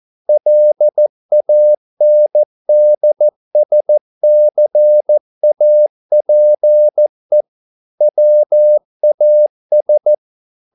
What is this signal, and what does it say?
Morse code, heard as a keyed tone, about 14 wpm, 600 Hz. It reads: LANDSCAPE WAS